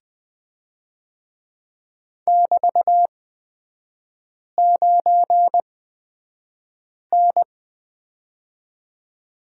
Morse code, heard as a keyed tone, about 20 wpm, spaced as for 5 wpm, 700 Hz. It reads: <BT>9N
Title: 224 Lesson 36e - BT - Three Characters 20wpm